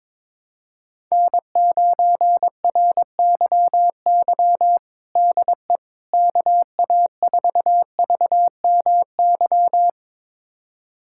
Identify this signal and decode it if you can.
Morse code, heard as a keyed tone, about 21 wpm, 700 Hz. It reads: N9RYY DE KA4VMY
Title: USA Callsign DE Callsign 22wpm